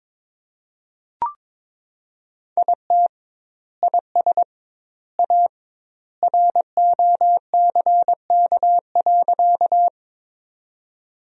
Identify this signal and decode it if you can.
Morse code, heard as a keyed tone, about 22 wpm, 700 Hz. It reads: IT IS A ROCK.
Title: 2x Word-Spacing - Sentences from Top 400 Words - 22wpm